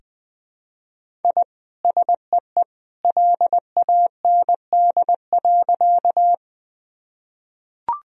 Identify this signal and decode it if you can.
Morse code, heard as a keyed tone, about 20 wpm, 700 Hz. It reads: I SEE LAND.
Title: Sentences from Top 1000 Words - Review Part 1 - 20wpm